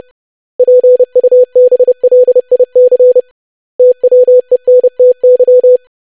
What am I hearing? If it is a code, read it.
Morse code, heard as a keyed tone, about 30 wpm, 500 Hz. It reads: PUBLIC TWENTY
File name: lesson / web / basic500-25wpm